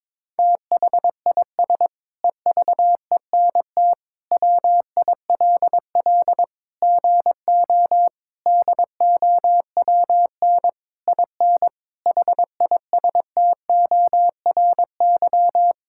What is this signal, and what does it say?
Morse code, heard as a keyed tone, about 22 wpm, 700 Hz. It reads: THIS EVENT WILL GO DOWN IN HISTORY
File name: Sentences with 9-Word Phrases - 22wpm